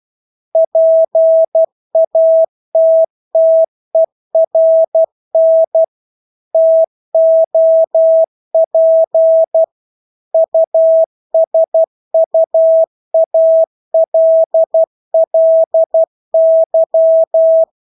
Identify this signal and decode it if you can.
Morse code, heard as a keyed tone, about 12 wpm, 650 Hz. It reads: PATTERN TOP USUALLY
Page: Engelske ord 12 wpm